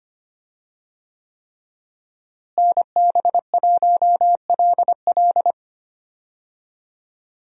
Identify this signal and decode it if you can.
Morse code, heard as a keyed tone, about 25 wpm, 700 Hz. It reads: NB1LL
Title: USA Call Signs - 5 Characters - Repeated 3x - 25wpm